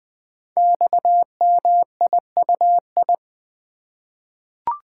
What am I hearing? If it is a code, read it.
Morse code, heard as a keyed tone, about 20 wpm, 700 Hz. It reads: XMIUI
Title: Code Groups - 5 Letters - 20wpm